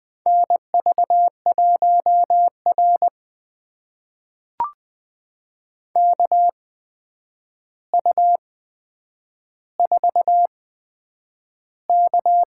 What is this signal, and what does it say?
Morse code, heard as a keyed tone, about 20 wpm, spaced as for 9 wpm, 700 Hz. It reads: NV1R KU4K
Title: USA Call Signs - 4 Characters - 8x Character Spacing - 20wpm